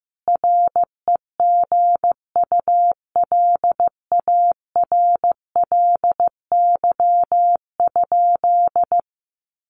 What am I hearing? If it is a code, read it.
Morse code, heard as a keyed tone, about 15 wpm, 700 Hz. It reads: REGULARLY?
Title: Sentences from Top 1400 Words 15wpm